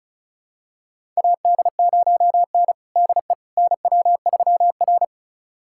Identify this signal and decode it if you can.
Morse code, heard as a keyed tone, about 36 wpm, 690 Hz. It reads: AD0N DE NW3R
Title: USA Callsign DE Callsign - 4 Characters - 35wpm